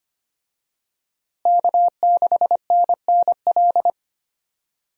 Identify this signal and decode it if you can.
Morse code, heard as a keyed tone, about 25 wpm, 700 Hz. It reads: K6NNL
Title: USA Call Signs - 5 Characters - Repeated 3x - 25wpm